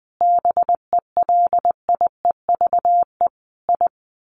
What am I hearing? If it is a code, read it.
Morse code, heard as a keyed tone, about 20 wpm, 700 Hz. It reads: BELIEVE I
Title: Sets of 5 Words Encore2 20wpm